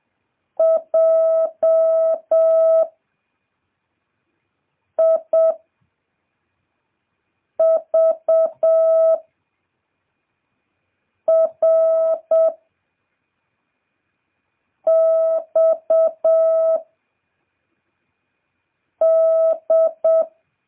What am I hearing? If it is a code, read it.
Morse code, heard as a keyed tone, about 7 wpm, 650 Hz. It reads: J I V R X D